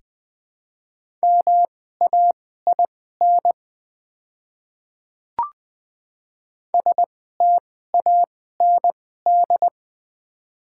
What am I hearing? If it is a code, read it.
Morse code, heard as a keyed tone, about 20 wpm, spaced as for 14 wpm, 700 Hz. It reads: MAIN STAND
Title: 2x Character Spacing - Top 500 Words - 20wpm